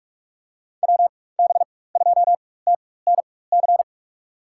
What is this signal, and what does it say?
Morse code, heard as a keyed tone, about 46 wpm, spaced as for 21 wpm, 700 Hz. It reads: WB2TNC
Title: USA Call Signs - 4x Character Spacing - 45wpm